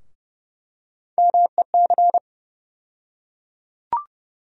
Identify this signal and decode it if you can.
Morse code, heard as a keyed tone, about 32 wpm, 700 Hz. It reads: MEC
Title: USA Call Sign Postfixes - 3 Letters 30wpm